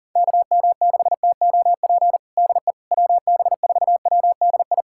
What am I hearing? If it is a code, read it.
Morse code, heard as a keyed tone, about 40 wpm, 700 Hz. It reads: KM6TOP DE WB4WDI
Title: USA Callsign DE Callsign - 6 Characters - 40wpm